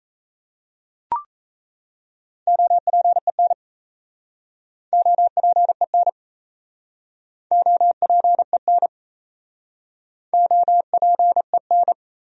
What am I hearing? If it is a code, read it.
Morse code, heard as a keyed tone, about 37 wpm, 700 Hz. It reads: OPEN OPEN OPEN OPEN